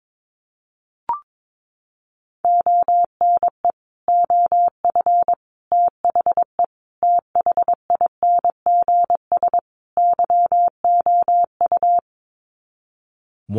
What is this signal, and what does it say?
Morse code, heard as a keyed tone, about 22 wpm, 700 Hz. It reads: ONE OF THE THINGS YOU
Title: Sets of 5 Words Encore2 22wpm